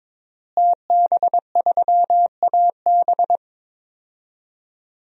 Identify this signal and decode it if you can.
Morse code, heard as a keyed tone, about 22 wpm, 700 Hz. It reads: TB3AB
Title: DXCC-100 Call Signs - 5 Characters - Repeated 3x - 22wpm